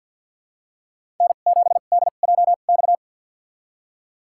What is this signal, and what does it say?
Morse code, heard as a keyed tone, about 52 wpm, spaced as for 36 wpm, 700 Hz. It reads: N7DJX
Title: USA Call Signs - 5 Characters - 2x Character Spacing - 50wpm